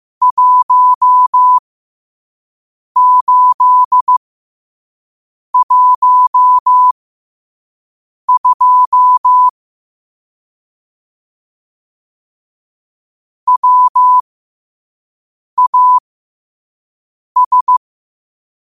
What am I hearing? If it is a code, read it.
Morse code, heard as a keyed tone, about 15 wpm, spaced as for 5 wpm, 1000 Hz. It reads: 1812 WAS
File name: qotdfiles / 5wpm